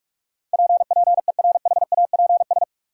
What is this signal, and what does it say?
Morse code, heard as a keyed tone, about 42 wpm, 700 Hz. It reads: PPERHAPS
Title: Sentences from Top 1100 Words - New Word Drill Alt - 45wpm